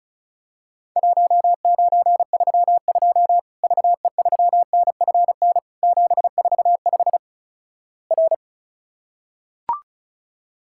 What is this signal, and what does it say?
Morse code, heard as a keyed tone, about 36 wpm, 700 Hz. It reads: 1932 VE3NFN 745 R
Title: Worked All Europe DX Contest - QTC Exchange - 35wpm